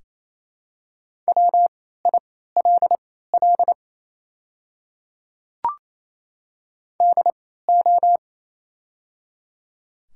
Morse code, heard as a keyed tone, about 27 wpm, spaced as for 16 wpm, 690 Hz. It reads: WILL DO
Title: 3x Character Spacing - Top 100 Words - 28wpm